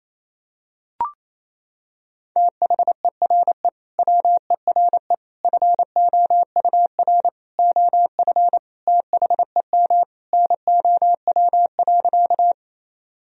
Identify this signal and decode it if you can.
Morse code, heard as a keyed tone, about 28 wpm, 700 Hz. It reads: THERE WERE FOUR OF THEM NOW.
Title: Sentences from Top 1000 Words - Review Part 1 - 28wpm